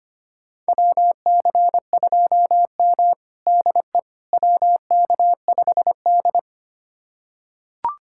Morse code, heard as a keyed tone, about 25 wpm, 700 Hz. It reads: WC2M DE WK5D
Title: USA Callsign DE Callsign - 4 Characters - 25wpm